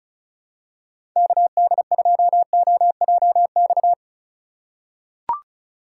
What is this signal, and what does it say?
Morse code, heard as a keyed tone, about 34 wpm, 690 Hz. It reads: KD2OJX
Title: USA Call Signs - 6 Characters - 6x Character Spacing - 35wpm